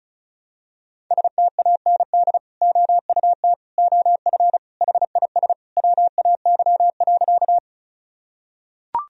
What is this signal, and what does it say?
Morse code, heard as a keyed tone, about 36 wpm, 700 Hz. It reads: STAND OUT OF HIS WAY.
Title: Sentences from Top 1000 Words - Review Part 1 - 35wpm